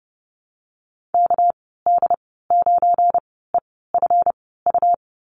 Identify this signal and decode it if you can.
Morse code, heard as a keyed tone, about 30 wpm, spaced as for 17 wpm, 700 Hz. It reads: KD9EFU
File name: Lesson 37g - Letter x - Call Signs 30wpm